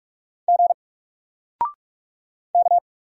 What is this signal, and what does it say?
Morse code, heard as a keyed tone, about 44 wpm, 700 Hz. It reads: G K
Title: Single Letter-Number 45wpm